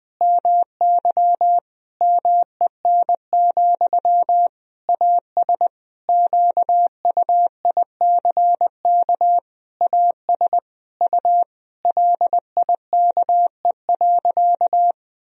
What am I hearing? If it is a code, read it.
Morse code, heard as a keyed tone, about 20 wpm, 700 Hz. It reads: MY MEN, AS QUICK AS U LIKE.
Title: Sentences from Top 500 Words and Abbreviations 20wpm